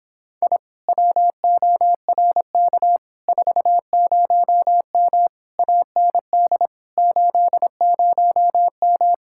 Morse code, heard as a keyed tone, about 26 wpm, 700 Hz. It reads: I WORK 40M AND 80M